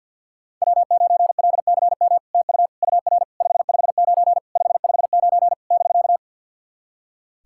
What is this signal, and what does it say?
Morse code, heard as a keyed tone, about 50 wpm, 700 Hz. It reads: W9LCM TU UR 559 559 <BK>
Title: SOTA QSO Practice 50wpm